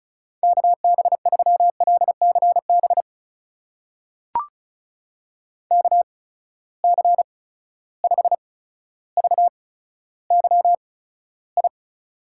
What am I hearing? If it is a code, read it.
Morse code, heard as a keyed tone, about 36 wpm, spaced as for 15 wpm, 700 Hz. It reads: KB3LCB KC5VYI
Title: USA Call Signs - 6 Characters - 8x Character Spacing - 35wpm